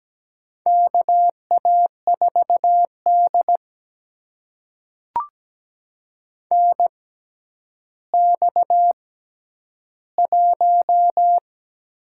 Morse code, heard as a keyed tone, about 17 wpm, spaced as for 10 wpm, 700 Hz. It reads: KA4D NX1
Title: USA Call Signs - 4 Characters - 6x Character Spacing - 17wpm